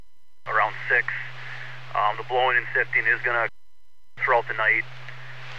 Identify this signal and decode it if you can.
Emergency Alert System (SAME) tones, heard as harsh buzzing bursts, with speech. It is heard from 0.8 to 4.9 s.